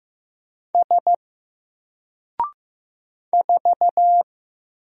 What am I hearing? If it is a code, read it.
Morse code, heard as a keyed tone, about 15 wpm, 700 Hz. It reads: S 4
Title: Single Character (Letter:Number:Punctuation:Prosign) Alt 15wpm